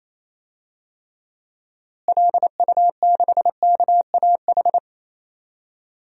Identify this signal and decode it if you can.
Morse code, heard as a keyed tone, about 28 wpm, 700 Hz. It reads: LU6KAH
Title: DXCC-100 Call Signs - 6 Characters - Repeated 2x - 28wpm